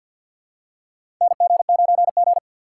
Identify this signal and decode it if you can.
Morse code, heard as a keyed tone, about 47 wpm, 700 Hz. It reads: NG9G
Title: USA Call Signs - 2x Character Spacing - 50wpm